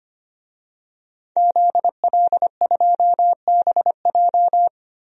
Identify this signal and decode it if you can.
Morse code, heard as a keyed tone, about 25 wpm, 700 Hz. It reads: ZL2BJ
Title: DXCC-100 Call Signs - 5 Characters - 25wpm